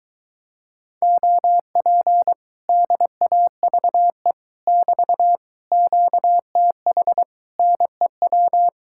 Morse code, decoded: OP DAVE <BT> QTH NEW

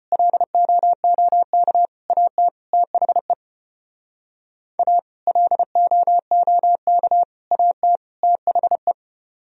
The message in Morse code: LOOK AT THE A LOOK AT THE

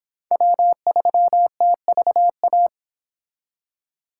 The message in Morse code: W3TVA

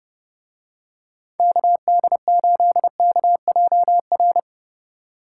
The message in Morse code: KD8KJR

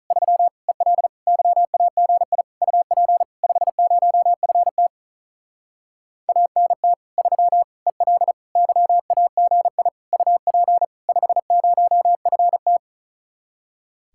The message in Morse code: 3 EL YAGI UP 50FT ANT 3 EL YAGI UP 50FT

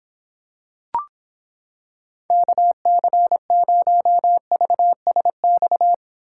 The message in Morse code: KC0VSX

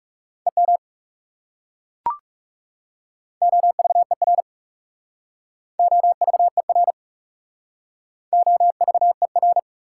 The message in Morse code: EM OVER OVER OVER

